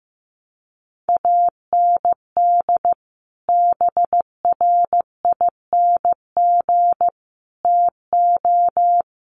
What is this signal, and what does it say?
Morse code, heard as a keyed tone, about 15 wpm, 700 Hz. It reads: AND BRING TO
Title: Sets of 5 Words Encore 15wpm